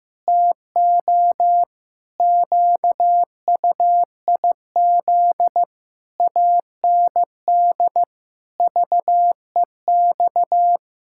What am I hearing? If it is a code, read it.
Morse code, heard as a keyed tone, about 15 wpm, 700 Hz. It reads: TO QUIZ AND VEX
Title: Pangram Challenge v2 - 15wpm